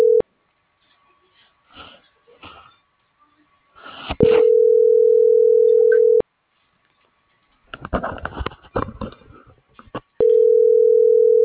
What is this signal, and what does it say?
Ringback tone (North America), 440 + 480 Hz.